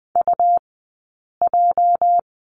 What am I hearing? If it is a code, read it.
Morse code, heard as a keyed tone, about 20 wpm, 700 Hz. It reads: U J
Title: 217 Lesson 35d - Number 0 - Two Characters 20wpm